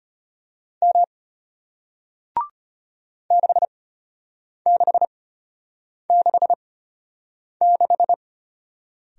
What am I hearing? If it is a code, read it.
Morse code, heard as a keyed tone, about 33 wpm, spaced as for 8 wpm, 700 Hz. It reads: M 6666